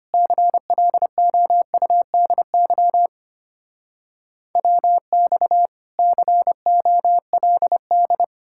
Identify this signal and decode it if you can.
Morse code, heard as a keyed tone, about 33 wpm, 700 Hz. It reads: CLOUDY WX COLD